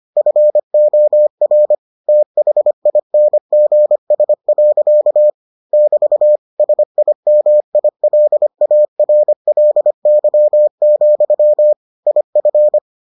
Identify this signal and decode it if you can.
Morse code, heard as a keyed tone, about 25 wpm, 600 Hz. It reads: FOR THINGS. <BT> SIMILARLY, IF